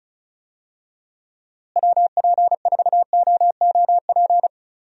Morse code, heard as a keyed tone, about 36 wpm, 700 Hz. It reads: WP4OOP
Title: USA Call Signs - 6 Characters - Repeated 3x - 35wpm